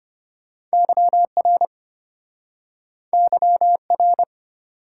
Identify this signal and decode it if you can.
Morse code, heard as a keyed tone, about 34 wpm, 700 Hz. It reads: YR YR